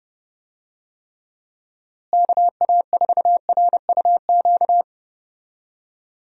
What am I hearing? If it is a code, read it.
Morse code, heard as a keyed tone, about 30 wpm, 700 Hz. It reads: KA4RUQ
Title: USA Call Signs - 6 Characters - Repeated 2x - 30wpm